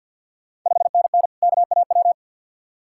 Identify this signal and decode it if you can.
Morse code, heard as a keyed tone, about 50 wpm, 690 Hz. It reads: 5NN KAW
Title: Ontario QSO Party Exchange 50wpm